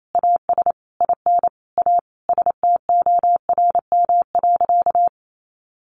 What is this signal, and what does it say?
Morse code, heard as a keyed tone, about 28 wpm, 700 Hz. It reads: AS IN A STORM.
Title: Sentences from Top 1500 Words 28wpm